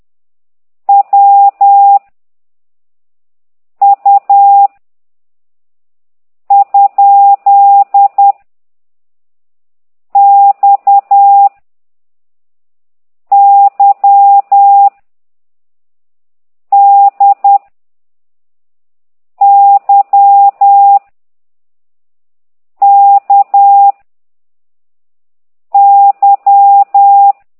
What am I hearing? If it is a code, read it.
Morse code, heard as a keyed tone, about 10 wpm, 800 Hz. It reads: W U ? X Y D Y K Y
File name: ct2idw / exerc9